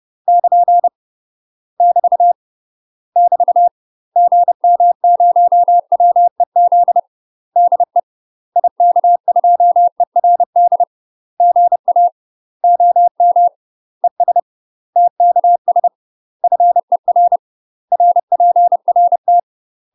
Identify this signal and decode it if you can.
Morse code, heard as a keyed tone, about 30 wpm, 700 Hz. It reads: <KN> <BT> <BT> GM0WEZ DE IK2ERD GA OM ES TKS FER RPRT
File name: UK MASTER QSOs TRAINING / 30WPM / UK MASTER QSOs 0031